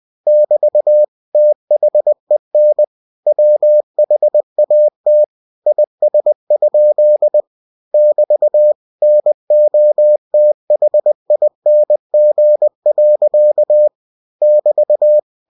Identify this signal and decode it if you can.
Morse code, heard as a keyed tone, about 20 wpm, 600 Hz. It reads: <BT> THEN WHAT IS? <BT> NOTHING. <BT>
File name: cw-books / winnie-the-pooh / winnie-20wpm-page-0027